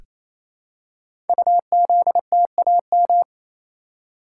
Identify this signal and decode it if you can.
Morse code, heard as a keyed tone, about 27 wpm, 700 Hz. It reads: UZTAM
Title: Code Groups - 5 Letters - 28wpm